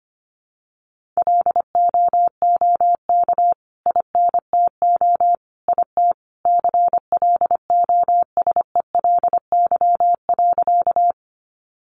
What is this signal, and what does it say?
Morse code, heard as a keyed tone, about 25 wpm, 700 Hz. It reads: LOOK INTO IT CLOSELY.
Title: Sentences from Top 1400 Words 25wpm